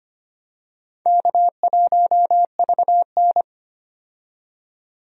Morse code, heard as a keyed tone, about 25 wpm, 700 Hz. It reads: K1VN
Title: USA Call Signs - 4 Characters - 25wpm